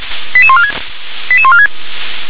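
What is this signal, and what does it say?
Five-tone selective call (ZVEI), heard as quick runs of five tones, tones of about 70 ms. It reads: 88146 88146
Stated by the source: ZVEI five-tone call 88631 received on an SDR.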